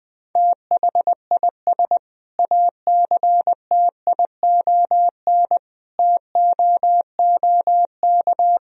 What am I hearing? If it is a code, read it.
Morse code, heard as a keyed tone, about 20 wpm, 700 Hz. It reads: THIS ACTION TOOK
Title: Sentences from Top 1000 Words - Review Part 4 - 20wpm